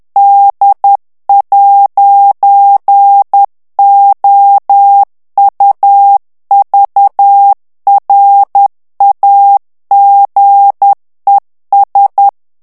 Morse code, heard as a keyed tone, about 10 wpm, 800 Hz. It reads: D'OUVRAGES
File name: morse mp3 / FRANCAIS TEXTE CLAIR 2 VITESSE 10 MOTS MINUTES